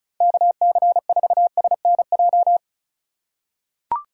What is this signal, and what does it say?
Morse code, heard as a keyed tone, about 34 wpm, 700 Hz. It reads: KC4SNJ
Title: USA Call Signs - 6 Characters - 2x Character Spacing - 35wpm